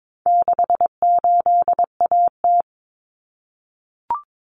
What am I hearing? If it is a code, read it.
Morse code, heard as a keyed tone, about 22 wpm, 700 Hz. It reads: 68AT
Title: Common Cut Numbers - 4-Digit 22wpm